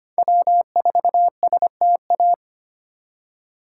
Morse code, heard as a keyed tone, about 25 wpm, 700 Hz. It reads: W4STA
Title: USA Call Signs - 5 Characters - 4x Character Spacing - 25wpm